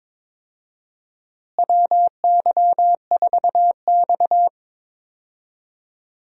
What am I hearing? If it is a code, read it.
Morse code, heard as a keyed tone, about 22 wpm, 700 Hz. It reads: WY4X
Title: USA Call Signs - 4 Characters - Repeated 2x - 22wpm